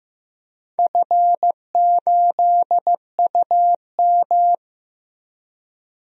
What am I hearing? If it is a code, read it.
Morse code, heard as a keyed tone, about 15 wpm, 700 Hz. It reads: F8UM
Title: DXCC-100 Call Signs - 4 Characters - Repeated 2x - 15wpm